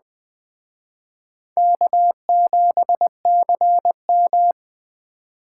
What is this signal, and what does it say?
Morse code, heard as a keyed tone, about 20 wpm, 700 Hz. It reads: K7CM
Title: Speed Racing - USA 1x2 Call Signs 17wpm